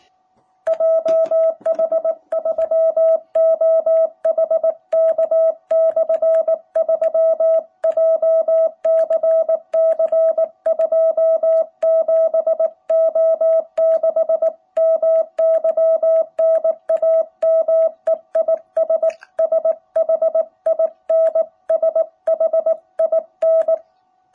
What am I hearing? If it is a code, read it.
Morse code, heard as a keyed tone, about 19 wpm, 660 Hz. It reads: JH3OHK/3 JCC 27O6 MY NAME IS SHIN SHIN
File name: jh8cbh / 2021cw